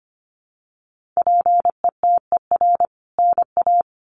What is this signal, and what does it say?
Morse code, heard as a keyed tone, about 25 wpm, 700 Hz. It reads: PETER NA
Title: CWT Exchange 25wpm